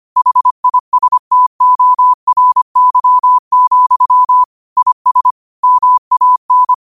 Morse code, heard as a keyed tone, about 25 wpm, 1000 Hz. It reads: HISTORY, IS MAN
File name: qotdfiles / 25wpm